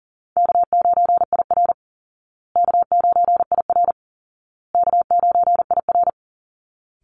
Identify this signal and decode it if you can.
Morse code, heard as a keyed tone, about 40 wpm, 700 Hz. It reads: K9IR K9IR K9IR
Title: Call Signs Repeated 2x - 40wpm